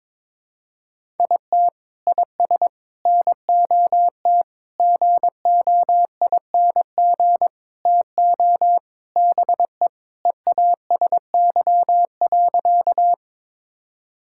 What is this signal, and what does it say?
Morse code, heard as a keyed tone, about 22 wpm, 700 Hz. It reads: IT IS NOT GOING TO BE EASY.
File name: practice / 22wpm / Sentences with 4-Word Phrases - 22wpm